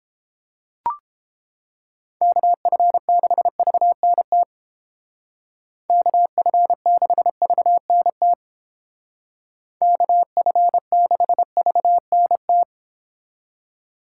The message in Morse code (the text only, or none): KF6VNT KF6VNT KF6VNT